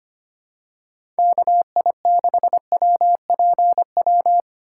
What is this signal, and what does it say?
Morse code, heard as a keyed tone, about 25 wpm, 700 Hz. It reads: KI6WPW
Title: USA Call Signs - 3x Character Spacing - 25wpm